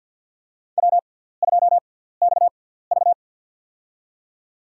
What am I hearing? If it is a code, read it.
Morse code, heard as a keyed tone, about 52 wpm, spaced as for 17 wpm, 700 Hz. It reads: W2XV
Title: USA Call Signs - 4 Characters - 6x Character Spacing - 50wpm